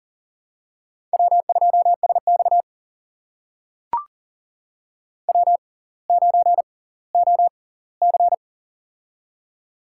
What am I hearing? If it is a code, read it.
Morse code, heard as a keyed tone, about 40 wpm, spaced as for 20 wpm, 700 Hz. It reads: W2SX W9OC
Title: USA Call Signs - 4 Characters - 6x Character Spacing - 40wpm